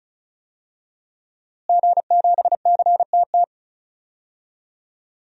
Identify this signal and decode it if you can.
Morse code, heard as a keyed tone, about 34 wpm, 700 Hz. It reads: G7CTT
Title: DXCC-100 Call Signs - 5 Characters - Repeated 3x - 35wpm